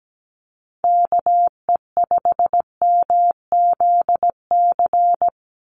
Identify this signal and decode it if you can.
Morse code, heard as a keyed tone, about 17 wpm, 700 Hz. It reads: KE5MZC
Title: USA Call Signs 17wpm